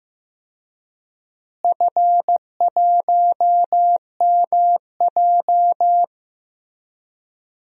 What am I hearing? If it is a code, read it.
Morse code, heard as a keyed tone, about 15 wpm, 700 Hz. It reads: F1MJ
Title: DXCC-100 Call Signs - 4 Characters - Repeated 3x - 15wpm